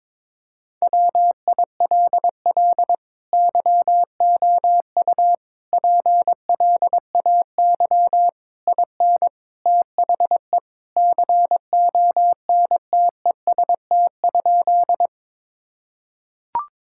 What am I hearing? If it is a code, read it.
Morse code, heard as a keyed tone, about 22 wpm, 700 Hz. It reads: WILL YOU PLAY IN THE CONTEST?
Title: Sentences from Top 1600 Words 22wpm